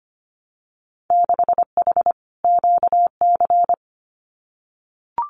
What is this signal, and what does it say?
Morse code, heard as a keyed tone, about 25 wpm, 700 Hz. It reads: 6H QC
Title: Winter Field Day Exchanges Encore 25wpm